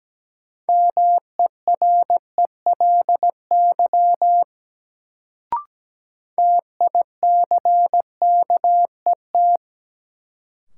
Morse code, heard as a keyed tone, about 17 wpm, 700 Hz. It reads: MERELY TICKET
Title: Sentences from Top 1400 Words - New Word Drill - 17wpm